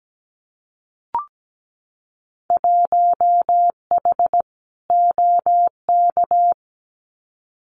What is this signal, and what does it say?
Morse code, heard as a keyed tone, about 17 wpm, 700 Hz. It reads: 1H OK
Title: Winter Field Day Exchanges 17wpm